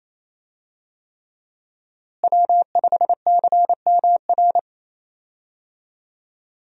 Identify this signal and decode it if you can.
Morse code, heard as a keyed tone, about 28 wpm, 700 Hz. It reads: W5CMR
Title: USA Call Signs - 5 Characters - Repeated 3x - 28wpm